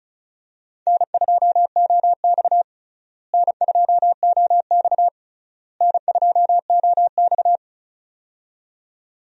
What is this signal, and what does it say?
Morse code, heard as a keyed tone, about 36 wpm, 700 Hz. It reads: N2OX N2OX N2OX
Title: USA Call Signs - 4 Characters - Repeated 3x - 35wpm